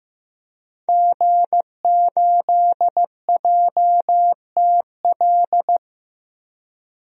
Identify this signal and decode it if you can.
Morse code, heard as a keyed tone, about 15 wpm, 700 Hz. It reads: G8JTL
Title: DXCC-100 Call Signs - 5 Characters - 15wpm